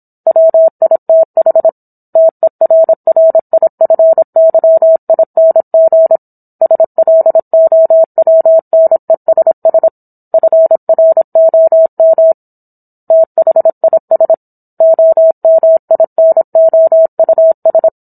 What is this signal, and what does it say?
Morse code, heard as a keyed tone, about 27 wpm, 650 Hz. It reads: WITH TERRIFYING SLOWNESS FROM THIS OMINOUS
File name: Book One / 17-Chapter 17 - 26 WPM